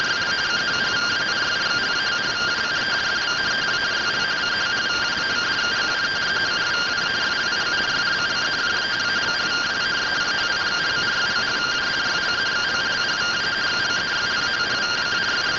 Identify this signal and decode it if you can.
Two-tone teleprinter signal, about 170 Hz shift.
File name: findings / MR AXIAL 71mm 3mm.gen-07 rtty rx